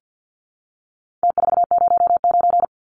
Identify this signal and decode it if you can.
Morse code, heard as a keyed tone, about 50 wpm, 700 Hz. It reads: T409
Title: Numbers - Contest Style - 4-Digit 50wpm